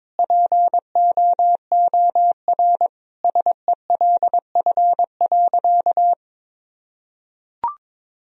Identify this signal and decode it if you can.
Morse code, heard as a keyed tone, about 22 wpm, 700 Hz. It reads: POOR SELF.
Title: Sentences from Top 1000 Words - Review Part 1 - 22wpm